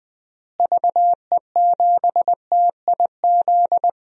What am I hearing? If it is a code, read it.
Morse code, heard as a keyed tone, about 20 wpm, 700 Hz. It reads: VE7TIZ
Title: DXCC-100 Call Signs - 6 Characters - Repeated 2x - 20wpm